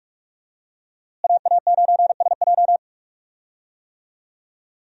Morse code, heard as a keyed tone, about 46 wpm, 700 Hz. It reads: AA9SJ
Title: USA Call Signs - 5 Characters - Repeated 2x - 45wpm